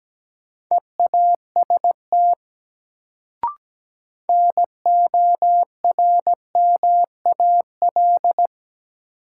Morse code, heard as a keyed tone, about 17 wpm, 700 Hz. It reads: EAST NORMAL
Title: Sentences from Top 1200 Words - New Word Drill Alt - 17wpm